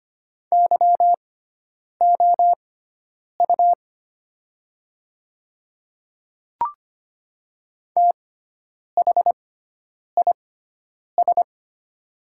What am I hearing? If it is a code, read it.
Morse code, heard as a keyed tone, about 25 wpm, spaced as for 9 wpm, 690 Hz. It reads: YOU THIS
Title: 6x Character Spacing - Top 100 Words - 25wpm